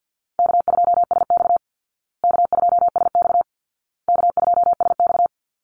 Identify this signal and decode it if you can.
Morse code, heard as a keyed tone, about 50 wpm, 700 Hz. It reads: K2SX K2SX K2SX
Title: Call Signs Repeated 2x - 50wpm